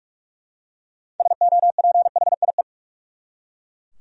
Morse code, heard as a keyed tone, about 41 wpm, 700 Hz. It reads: SOPHIE
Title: USA Common Names - 45wpm